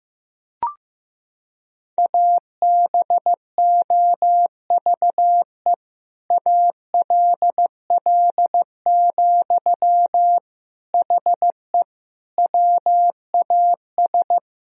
Morse code, heard as a keyed tone, about 15 wpm, 700 Hz. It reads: ABOVE ALL, HE WAS
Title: Sentences from Top 1000 Words - Review Part 2 - 15wpm